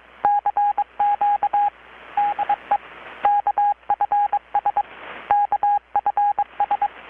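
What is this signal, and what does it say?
Morse code, heard as a keyed tone, about 22 wpm, 820 Hz, with radio noise. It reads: CQ DE KFS KFS
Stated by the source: Morse from KPH's Night of Nights XX maritime radiotelegraphy event, received in Chiapas, Mexico about 3,800 km away on a Kenwood R-600 with a grounded 10 m long-wire antenna.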